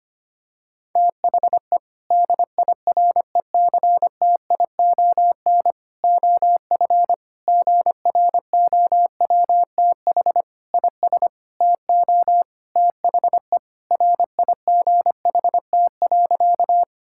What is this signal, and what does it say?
Morse code, heard as a keyed tone, about 25 wpm, 700 Hz. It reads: THE DIRECTION OF GROWTH IS TO THE RIGHT.
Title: Sentences from Top 1000 Words - Review Part 5 - 25wpm